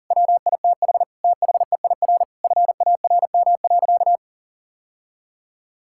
Morse code, heard as a keyed tone, about 40 wpm, 700 Hz. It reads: WITH THEIR FARM.
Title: Sentences from Top 1000 Words - Review Part 6 - 40wpm